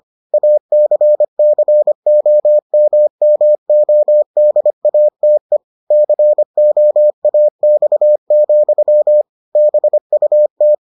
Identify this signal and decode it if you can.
Morse code, heard as a keyed tone, about 25 wpm, 600 Hz. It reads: ACCOMMODATE COAX, BUT